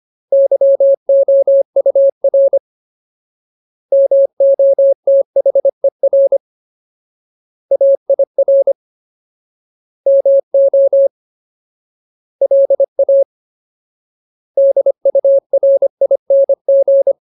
Morse code, heard as a keyed tone, about 25 wpm, 550 Hz. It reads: YOUR MOTHER AIR MO LA DURING
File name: Ultimate Word List / 25WPM Ultimate